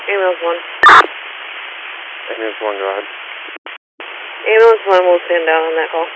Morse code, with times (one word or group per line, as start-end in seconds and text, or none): none